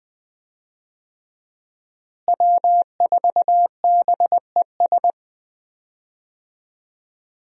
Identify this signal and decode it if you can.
Morse code, heard as a keyed tone, about 20 wpm, 700 Hz. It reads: W4BES
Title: USA Call Signs - 5 Characters - Repeated 2x - 20wpm